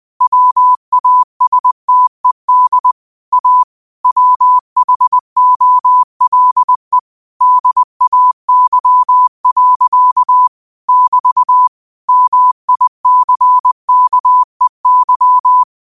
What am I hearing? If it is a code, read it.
Morse code, heard as a keyed tone, about 20 wpm, 1000 Hz. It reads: WASTED A WHOLE DAY. <BT> MICKEY